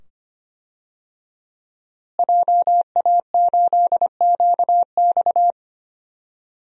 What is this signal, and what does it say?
Morse code, heard as a keyed tone, about 25 wpm, 700 Hz. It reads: JA8QX
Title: DXCC-100 Call Signs - 5 Characters - Repeated 2x - 25wpm